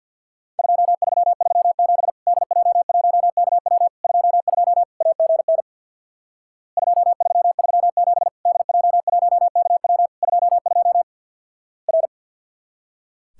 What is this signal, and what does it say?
Morse code, heard as a keyed tone, about 50 wpm, 700 Hz. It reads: 2337 DJ1KW 22 AGN 2337 DJ1KW 22 R
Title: Worked All Europe DX Contest - QTC Exchange with Repeats - 50wpm